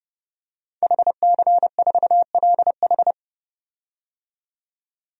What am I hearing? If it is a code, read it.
Morse code, heard as a keyed tone, about 30 wpm, 700 Hz. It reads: HC4LH
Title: DXCC-100 Call Signs - 5 Characters - 30wpm